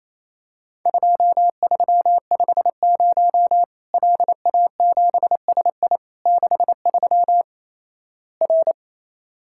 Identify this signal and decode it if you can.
Morse code, heard as a keyed tone, about 28 wpm, 700 Hz. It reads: 2350 LA7SI 63 R
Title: Worked All Europe DX Contest - QTC Exchange - 28wpm